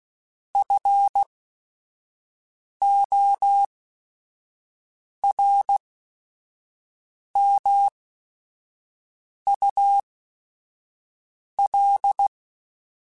Morse code, heard as a keyed tone, about 16 wpm, spaced as for 5 wpm, 800 Hz. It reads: FORMUL
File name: Technical Subjects / Technical Subjects-mp3s / XC @5